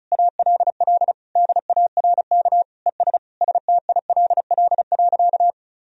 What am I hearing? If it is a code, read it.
Morse code, heard as a keyed tone, about 36 wpm, 700 Hz. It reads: ALL DARK ES STILL.
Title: Sentences from Top 500 Words and Abbreviations 35wpm